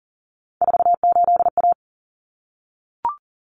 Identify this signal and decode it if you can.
Morse code, heard as a keyed tone, about 37 wpm, 700 Hz. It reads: 48A
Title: Common Cut Numbers - 3-Digit 40wpm